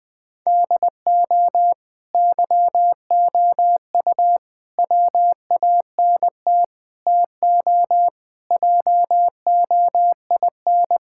Morse code, heard as a keyed tone, about 20 wpm, 700 Hz. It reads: DO YOU WANT TO JOIN
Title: Sentences with 4-Word Phrases - 20wpm